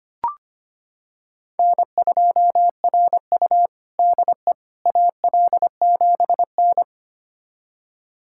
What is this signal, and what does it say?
Morse code, heard as a keyed tone, about 25 wpm, 700 Hz. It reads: N2RU DE AL7N
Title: USA Callsign DE Callsign - 4 Characters - 25wpm